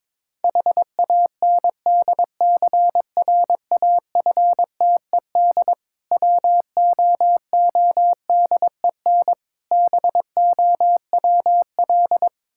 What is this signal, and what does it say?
Morse code, heard as a keyed tone, about 22 wpm, 690 Hz. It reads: HANDCRAFTED WOODEN BOWL
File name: Difficult Words with Hints - 22wpm